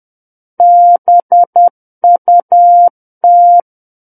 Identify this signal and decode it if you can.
Morse code, heard as a keyed tone, about 10 wpm, 700 Hz. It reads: BUT
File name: Book One / 01-Chapter 1 - 10 WPM